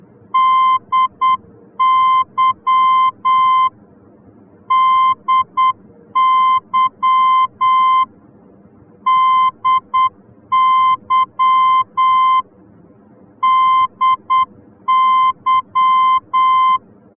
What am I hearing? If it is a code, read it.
Morse code, heard as a keyed tone, about 8 wpm, 1040 Hz. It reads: DY DY DY DY
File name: germany / mx / DY-D